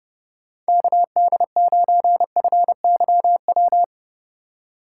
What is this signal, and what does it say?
Morse code, heard as a keyed tone, about 30 wpm, 700 Hz. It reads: KD9FYW
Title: USA Call Signs - 6 Characters - 8x Character Spacing - 30wpm